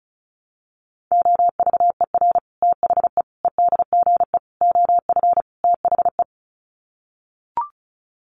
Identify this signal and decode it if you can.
Morse code, heard as a keyed tone, about 34 wpm, 700 Hz. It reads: OVER THE EDGE OF THE
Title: Sets of 5 Words Encore2 35wpm